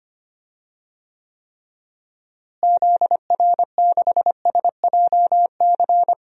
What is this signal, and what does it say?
Morse code, heard as a keyed tone, about 25 wpm, 700 Hz. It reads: ZR6SJC
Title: DXCC-100 Call Signs - 6 Characters - Repeated 3x - 25wpm